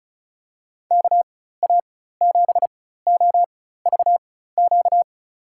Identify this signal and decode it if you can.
Morse code, heard as a keyed tone, about 36 wpm, spaced as for 16 wpm, 700 Hz. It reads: KA7OVQ
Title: USA Call Signs - 6 Characters - 4x Character Spacing - 35wpm